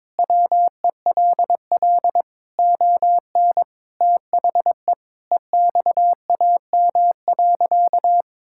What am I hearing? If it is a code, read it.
Morse code, heard as a keyed tone, about 22 wpm, 700 Hz. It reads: WELL ON THE EXAM.
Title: Sentences from Top 1000 Words - Review Part 5 - 22wpm